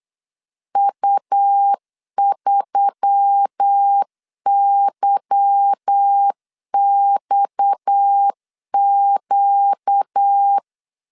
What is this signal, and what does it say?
Morse code, heard as a keyed tone, about 8 wpm, 800 Hz. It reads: U3YXQ
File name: LEZIONI / 8 YJ45 / 8 Y 8wpm 3245